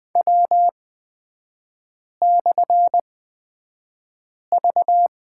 Morse code, heard as a keyed tone, about 20 wpm, spaced as for 5 wpm, 700 Hz. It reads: W/V
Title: 224 Lesson 36e - BT - Three Characters 20wpm